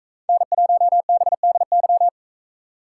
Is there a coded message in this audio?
Yes: Morse code from 0.3 to 2.1 s, heard as a keyed tone, about 45 wpm, 700 Hz. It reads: N1BDY